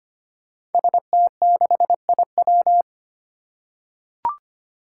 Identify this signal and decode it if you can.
Morse code, heard as a keyed tone, about 25 wpm, 700 Hz. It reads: ST6IW
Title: Five Challenging Letter-Number Combinations 25wpm